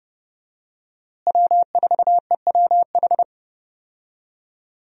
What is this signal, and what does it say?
Morse code, heard as a keyed tone, about 30 wpm, 700 Hz. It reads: W4EWH
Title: USA Call Signs - 5 Characters - Repeated 3x - 30wpm